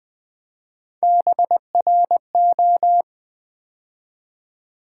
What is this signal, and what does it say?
Morse code, heard as a keyed tone, about 20 wpm, 700 Hz. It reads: BRO
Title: Build A Word - 3 Letters At A Time - 9-Letter Words - 20wpm